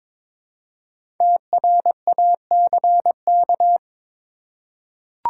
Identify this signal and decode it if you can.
Morse code, heard as a keyed tone, about 22 wpm, 700 Hz. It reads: TRACK
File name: 5 Letter Words - First 3 Letters Given - 22wpm